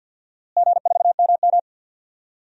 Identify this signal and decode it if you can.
Morse code, heard as a keyed tone, about 51 wpm, 700 Hz. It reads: G4MM
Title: DXCC-100 Call Signs - 4 Characters - Repeated 2x - 50wpm